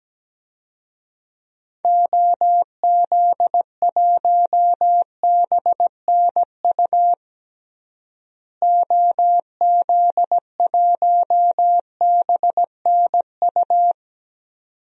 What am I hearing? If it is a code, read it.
Morse code, heard as a keyed tone, about 17 wpm, 700 Hz. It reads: OZ1BNU OZ1BNU
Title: DXCC-100 Call Signs - 6 Characters - Repeated 2x - 17wpm